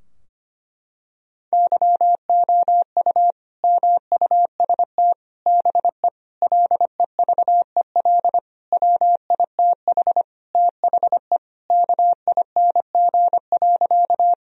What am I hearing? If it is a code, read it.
Morse code, heard as a keyed tone, about 25 wpm, 700 Hz. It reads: YOU MUST BE LEVEL WITH THE KING.
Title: Sentences from Top 1000 Words - Review Part 4 - 25wpm